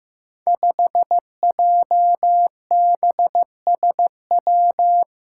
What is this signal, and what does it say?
Morse code, heard as a keyed tone, about 15 wpm, 700 Hz. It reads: 5JBSW
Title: Five Challenging Letter-Number Combinations 15wpm